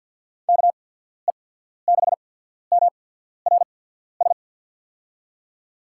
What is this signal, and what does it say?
Morse code, heard as a keyed tone, about 47 wpm, spaced as for 14 wpm, 700 Hz. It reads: KE6MRS
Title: USA Call Signs - 6 Characters - 8x Character Spacing - 50wpm